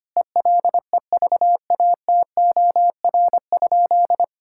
Morse code, heard as a keyed tone, about 25 wpm, 700 Hz. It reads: ELEVATOR?
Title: Sentences from Top 1600 Words 25wpm